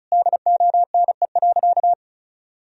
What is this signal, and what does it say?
Morse code, heard as a keyed tone, about 36 wpm, 700 Hz. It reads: DONE.
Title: Sentences from Top 1000 Words - Review Part 1 - 35wpm